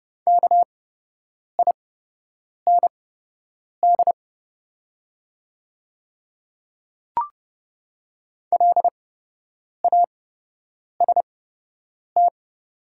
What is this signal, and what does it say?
Morse code, heard as a keyed tone, about 30 wpm, spaced as for 8 wpm, 700 Hz. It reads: KIND LAST
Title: 8x Character Spacing - Top 300 Words - 30wpm